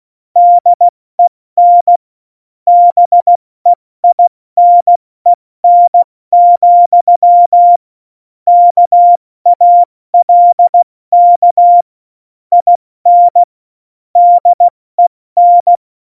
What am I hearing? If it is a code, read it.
Morse code, heard as a keyed tone, about 16 wpm, 700 Hz. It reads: DEN BEINEN, KALK IN DEN
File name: Morsekurs / Witze / Witz 20. Februar 2018, 70 F80